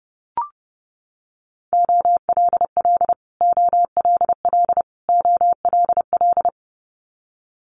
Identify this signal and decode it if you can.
Morse code, heard as a keyed tone, about 30 wpm, 700 Hz. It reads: OLL OLL OLL
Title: Multi-letter Phonemes Repeated 3x 30wpm